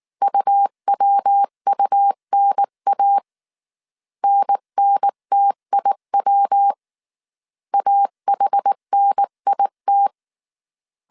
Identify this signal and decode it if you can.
Morse code, heard as a keyed tone, about 19 wpm, 800 Hz. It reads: UWUNA NNTIW AHNIT
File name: CW / DET / 1 3 1500C S B 18wpm 1916